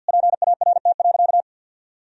Morse code, heard as a keyed tone, about 49 wpm, 700 Hz. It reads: PART.
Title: Sentences from Top 1000 Words - Review Part 1 - 50wpm